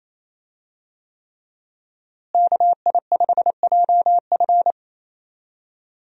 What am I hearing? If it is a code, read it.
Morse code, heard as a keyed tone, about 27 wpm, 700 Hz. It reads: KI5JF
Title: USA Call Signs - 5 Characters - Repeated 3x - 28wpm